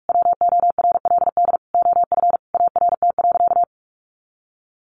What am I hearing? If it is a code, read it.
Morse code, heard as a keyed tone, about 44 wpm, 700 Hz. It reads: WORLD OF ART.
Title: Sentences from Top 1500 Words 45wpm